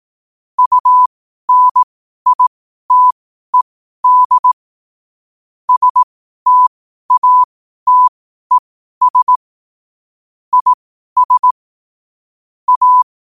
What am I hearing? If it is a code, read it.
Morse code, heard as a keyed tone, about 18 wpm, spaced as for 12 wpm, 1000 Hz. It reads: UNITED STATES IS A
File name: qotdfiles / 12wpm